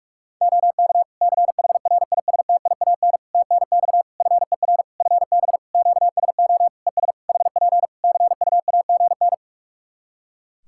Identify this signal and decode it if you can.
Morse code, heard as a keyed tone, about 46 wpm, 700 Hz. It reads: OK CHRISTIAN TNX FER FB QSO ES HP CUAGN
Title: Practice QSO - Exchange 4 - Ending - 45wpm